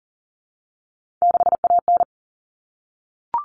Morse code, heard as a keyed tone, about 40 wpm, 700 Hz. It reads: 6AN
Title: Common Cut Numbers - 3-Digit 40wpm